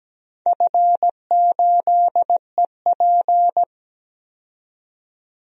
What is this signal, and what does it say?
Morse code, heard as a keyed tone, about 17 wpm, 700 Hz. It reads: F8EP
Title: DXCC-100 Call Signs - 4 Characters - 17wpm